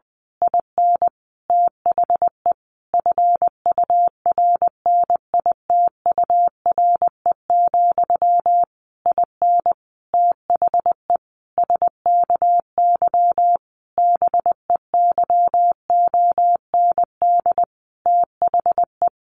Morse code, heard as a keyed tone, about 20 wpm, 700 Hz. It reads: IN THE FURNITURE, IN THE SKY BEYOND THE
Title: A Sound Of Thunder 20wpm